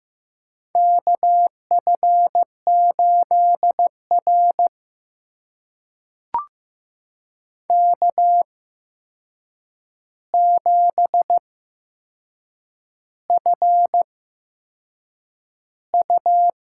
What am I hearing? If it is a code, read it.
Morse code, heard as a keyed tone, about 15 wpm, spaced as for 6 wpm, 700 Hz. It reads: KF8R K7FU
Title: USA Call Signs - 4 Characters - 8x Character Spacing - 15wpm